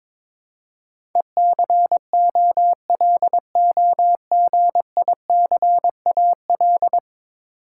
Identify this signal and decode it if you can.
Morse code, heard as a keyed tone, about 22 wpm, 700 Hz. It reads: ECOLOGICAL
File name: practice / 22wpm / 10 Letter Words - First 3 Letters Given - 22wpm